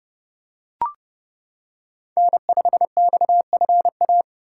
Morse code, heard as a keyed tone, about 29 wpm, 700 Hz. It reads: N5XFA